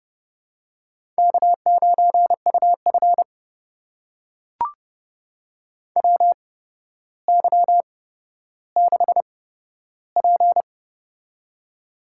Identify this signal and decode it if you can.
Morse code, heard as a keyed tone, about 32 wpm, spaced as for 13 wpm, 700 Hz. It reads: K9UF WY6P
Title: USA Call Signs - 4 Characters - 8x Character Spacing - 30wpm